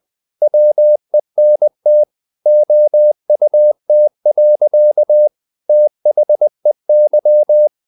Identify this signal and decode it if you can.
Morse code, heard as a keyed tone, about 20 wpm, 600 Hz. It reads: WENT OUT. THEY
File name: the old man and the sea / the old man and the sea part 1 20wpm